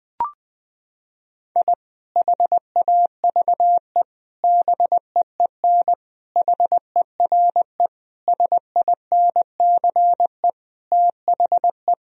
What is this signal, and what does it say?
Morse code, heard as a keyed tone, about 21 wpm, 700 Hz. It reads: I HAVE BEEN HERE SINCE THE
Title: Sentences from Top 1000 Words - Review Part 1 - 20wpm